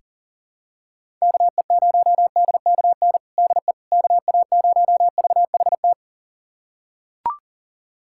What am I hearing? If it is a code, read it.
Morse code, heard as a keyed tone, about 40 wpm, 700 Hz. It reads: KE0DKN DE KA0VHT
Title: USA Callsign DE Callsign - 6 Characters - 40wpm